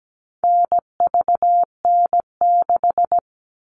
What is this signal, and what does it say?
Morse code, heard as a keyed tone, about 17 wpm, 700 Hz. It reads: NVN6
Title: All Cut Numbers - 4-Digit 17wpm